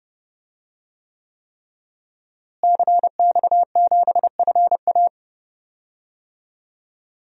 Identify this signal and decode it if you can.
Morse code, heard as a keyed tone, about 30 wpm, 700 Hz. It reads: CX7FA
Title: DXCC-100 Call Signs - 5 Characters - Repeated 3x - 30wpm